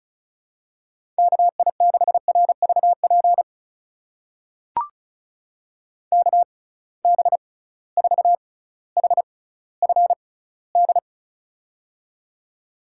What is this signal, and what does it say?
Morse code, heard as a keyed tone, about 36 wpm, spaced as for 18 wpm, 700 Hz. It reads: KI6RVP KB4HFD
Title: USA Call Signs - 6 Characters - 6x Character Spacing - 35wpm